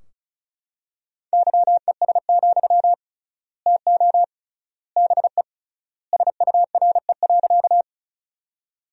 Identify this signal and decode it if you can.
Morse code, heard as a keyed tone, about 36 wpm, 700 Hz. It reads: YES, TO BE SURE.
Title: 3x Word-Spacing - Sentences from Top 500 Words - 35wpm